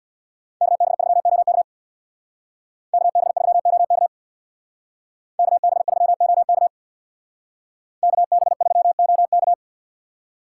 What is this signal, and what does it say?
Morse code, heard as a keyed tone, about 62 wpm, 700 Hz. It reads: KB3OK KB3OK KB3OK KB3OK
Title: Speed Racing - USA 2x2 Call Signs 50wpm